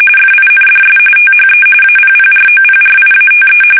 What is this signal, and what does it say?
RTTY, about 45.45 baud. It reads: FINDS THE APPROPRIATE